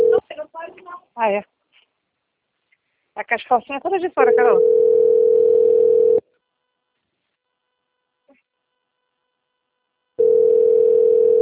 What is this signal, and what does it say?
Ringback tone (North America), 440 + 480 Hz, with speech.
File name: out- - - - - .73814